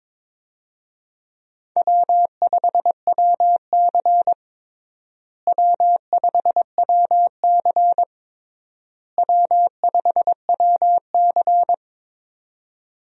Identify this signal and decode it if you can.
Morse code, heard as a keyed tone, about 22 wpm, 700 Hz. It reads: W5WC W5WC W5WC
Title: USA Call Signs - 4 Characters - Repeated 3x - 22wpm